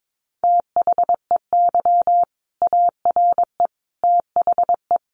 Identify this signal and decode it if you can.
Morse code, heard as a keyed tone, about 22 wpm, 700 Hz. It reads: THEY ARE THE
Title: Sentences from Top 1200 Words 22wpm